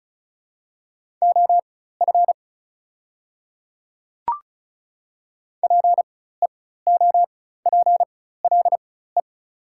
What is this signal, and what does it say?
Morse code, heard as a keyed tone, about 36 wpm, spaced as for 16 wpm, 700 Hz. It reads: OF PEOPLE
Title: 4x Character Spacing - Top 100 Words - 35wpm